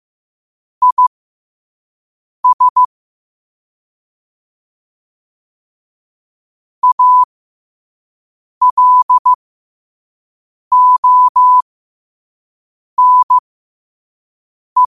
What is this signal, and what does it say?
Morse code, heard as a keyed tone, about 15 wpm, spaced as for 5 wpm, 1000 Hz. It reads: IS ALONE
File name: qotdfiles / 5wpm